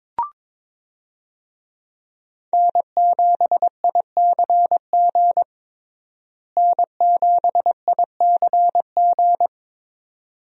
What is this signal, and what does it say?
Morse code, heard as a keyed tone, about 21 wpm, 700 Hz. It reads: N7ICG N7ICG
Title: USA Call Signs - 5 Characters - Repeated 2x - 22wpm